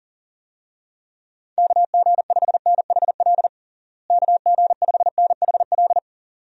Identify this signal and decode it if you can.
Morse code, heard as a keyed tone, about 40 wpm, 700 Hz. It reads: KG5NHL KG5NHL
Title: USA Call Signs - 6 Characters - Repeated 2x - 40wpm